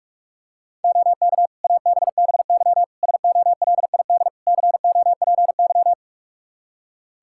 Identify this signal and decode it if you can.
Morse code, heard as a keyed tone, about 46 wpm, 700 Hz. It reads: OK ABBY SOLID COPY
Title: Practice QSO - Exchange 3 - More Info - 45wpm